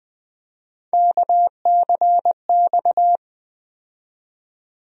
Morse code, heard as a keyed tone, about 21 wpm, 700 Hz. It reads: KCX
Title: USA Call Sign Postfixes - 3 Letters 20wpm